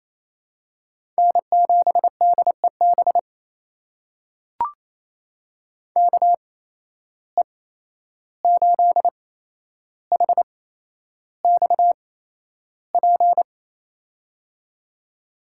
Morse code, heard as a keyed tone, about 27 wpm, spaced as for 11 wpm, 700 Hz. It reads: N7DEB KE8HXP
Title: USA Call Signs - 8x Character Spacing - 28wpm